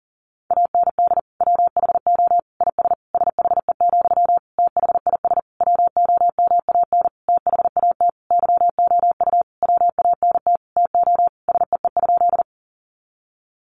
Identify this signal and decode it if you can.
Morse code, heard as a keyed tone, about 40 wpm, 700 Hz. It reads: AND WHO IS SHE, THIS WOMAN THAT YOU WANT TO SEE?
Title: Sentences from Top 1100 Words 40wpm